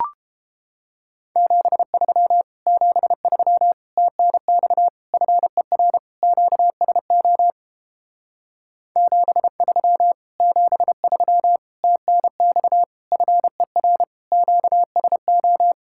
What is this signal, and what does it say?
Morse code, heard as a keyed tone, about 29 wpm, 700 Hz. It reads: 73 73 TNX FER QSO 73 73 TNX FER QSO